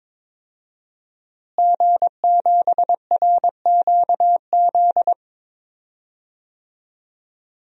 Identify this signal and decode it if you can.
Morse code, heard as a keyed tone, about 22 wpm, 700 Hz. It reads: G7RQZ
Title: DXCC-100 Call Signs - 5 Characters - Repeated 2x - 22wpm